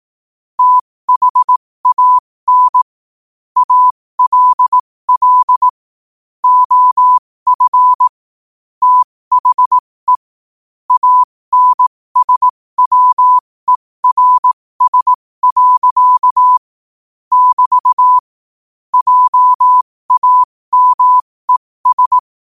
Morse code, heard as a keyed tone, about 18 wpm, spaced as for 15 wpm, 1000 Hz. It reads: THAN ALL OF THE ANSWERS. <BT> JAMES
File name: qotdfiles / 15wpm